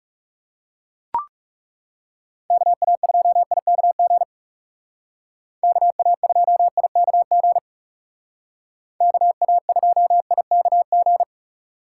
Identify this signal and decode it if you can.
Morse code, heard as a keyed tone, about 35 wpm, 700 Hz. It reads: KA2IKG KA2IKG KA2IKG
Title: Speed Racing - USA 2x3 Call Signs 30wpm